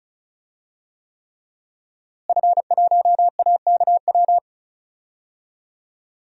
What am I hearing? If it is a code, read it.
Morse code, heard as a keyed tone, about 34 wpm, 700 Hz. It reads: F1AKW
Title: DXCC-100 Call Signs - 5 Characters - Repeated 3x - 35wpm